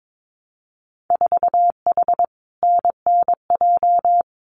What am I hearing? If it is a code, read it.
Morse code, heard as a keyed tone, about 22 wpm, 700 Hz. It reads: 4H NNJ
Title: Winter Field Day Exchanges Encore 22wpm